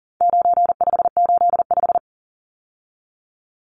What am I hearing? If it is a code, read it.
Morse code, heard as a keyed tone, about 40 wpm, 700 Hz. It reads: 9585
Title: Numbers - 4-Digit 40wpm